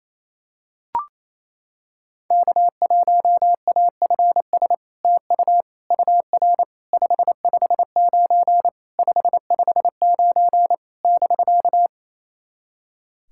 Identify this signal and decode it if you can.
Morse code, heard as a keyed tone, about 28 wpm, 700 Hz. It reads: K1AFS TU UR 559 559 <BK>
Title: SOTA QSO Practice 28wpm